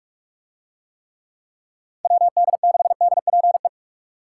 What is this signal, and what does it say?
Morse code, heard as a keyed tone, about 46 wpm, 700 Hz. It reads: WD6DPE